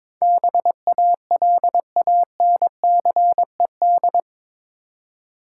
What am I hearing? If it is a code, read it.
Morse code, heard as a keyed tone, about 22 wpm, 700 Hz. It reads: BALANCED
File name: practice / 22wpm / Eight Letter Words - 22wpm